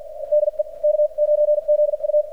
Morse code, heard as a keyed tone, about 50 wpm, spaced as for 42 wpm, 600 Hz, with radio noise. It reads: DE K9ZU
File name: archived / CRSnet